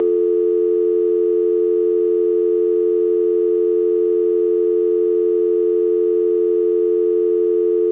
Dial tone (North America), 350 + 440 Hz.
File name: ringtones / Dial-Tone